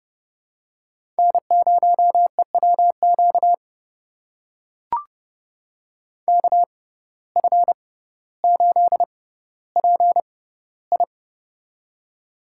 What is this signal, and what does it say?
Morse code, heard as a keyed tone, about 30 wpm, spaced as for 15 wpm, 700 Hz. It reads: N0EWQ KF8PI
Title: USA Call Signs - 5 Characters - 6x Character Spacing - 30wpm